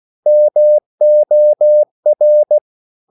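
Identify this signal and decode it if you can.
Morse code, heard as a keyed tone, about 16 wpm, 600 Hz. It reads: MOR
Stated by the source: Slow (QRS) Morse code at 16 wpm, from CW med Gnister.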